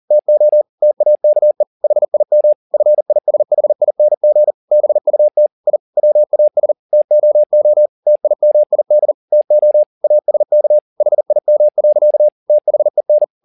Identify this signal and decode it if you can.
Morse code, heard as a keyed tone, about 40 wpm, 600 Hz. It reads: TO TAKE HIM FISHING BUT I WAS TOO TIMID TO ASK HIM. THEN